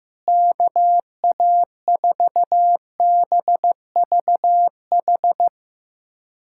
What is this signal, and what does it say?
Morse code, heard as a keyed tone, about 15 wpm, 700 Hz. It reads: KA4BVH
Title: USA Callsign DE Callsign - 6 Characters - 15wpm